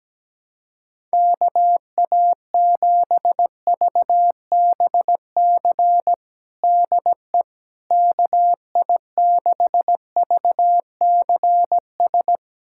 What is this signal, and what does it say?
Morse code, heard as a keyed tone, about 17 wpm, 700 Hz. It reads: KA7VBC DE KI6VCS
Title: USA Callsign DE Callsign - 6 Characters - 17wpm